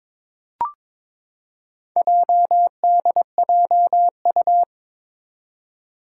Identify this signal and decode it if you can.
Morse code, heard as a keyed tone, about 22 wpm, 700 Hz. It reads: JDJU
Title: Four Challenging Letter-Number Combinations 22wpm